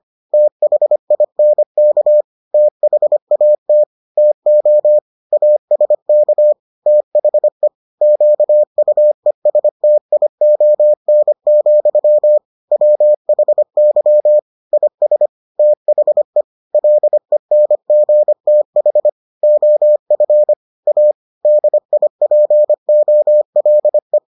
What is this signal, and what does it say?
Morse code, heard as a keyed tone, about 25 wpm, 600 Hz. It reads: THINK THAT TO ASK THE QUESTION, WHY IS THE LENGTH OF A DIPOLE